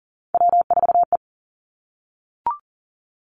Morse code, heard as a keyed tone, about 40 wpm, 700 Hz. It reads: W4E
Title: Three Challenging Letter-Number Combinations 40wpm